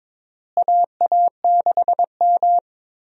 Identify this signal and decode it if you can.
Morse code, heard as a keyed tone, about 22 wpm, 700 Hz. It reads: AA6M
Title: USA Call Signs - 4 Characters - Repeated 3x - 22wpm